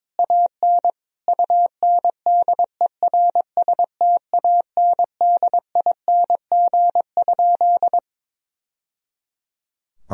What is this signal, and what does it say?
Morse code, heard as a keyed tone, about 22 wpm, 700 Hz. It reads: AN UNDERSTANDING?
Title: Sentences from Top 1000 Words - Review Part 3 - 22wpm